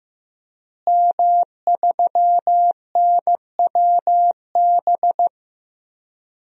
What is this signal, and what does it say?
Morse code, heard as a keyed tone, about 15 wpm, 700 Hz. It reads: M3NWB
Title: DXCC-100 Call Signs - 5 Characters - Repeated 3x - 15wpm